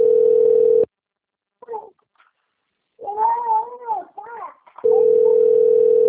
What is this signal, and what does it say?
Ringback tone (North America), 440 + 480 Hz, with speech.
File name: rg-1600-unknown- - - .26723